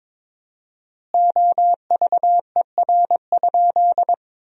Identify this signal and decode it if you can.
Morse code, heard as a keyed tone, about 21 wpm, 700 Hz. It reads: OVER?
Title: 4x Word-Spacing - Sentences from Top 100 Words - 22wpm